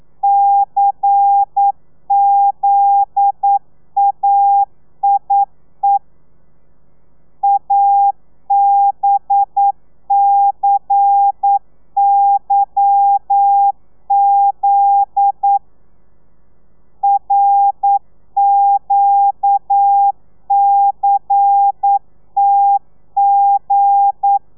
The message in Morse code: CZAIE ABCYZ RQCTG